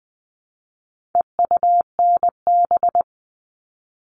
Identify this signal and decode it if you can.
Morse code, heard as a keyed tone, about 20 wpm, 700 Hz. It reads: EUNB
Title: All Cut Numbers - 4-Digit 20wpm